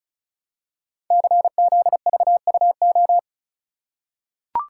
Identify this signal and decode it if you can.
Morse code, heard as a keyed tone, about 34 wpm, 700 Hz. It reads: CZVUO
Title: Code Groups - 5 Letter:Numbers - 35wpm